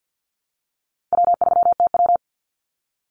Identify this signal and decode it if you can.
Morse code, heard as a keyed tone, about 50 wpm, 700 Hz. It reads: W3TW
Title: Call Signs - 50wpm